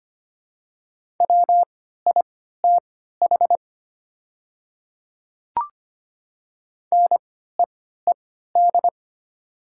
Morse code, heard as a keyed tone, about 25 wpm, spaced as for 14 wpm, 700 Hz. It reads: WITH NEED
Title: 3x Character Spacing - Top 300 Words - 25wpm